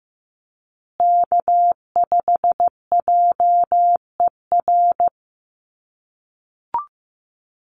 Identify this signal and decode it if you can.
Morse code, heard as a keyed tone, about 15 wpm, 700 Hz. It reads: K5JER
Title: USA Call Signs Repeated 3x 15wpm